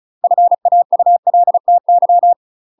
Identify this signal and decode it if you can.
Morse code, heard as a keyed tone, about 34 wpm, 700 Hz. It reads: FAULTY
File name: UK MASTER QSOs TRAINING/35WPM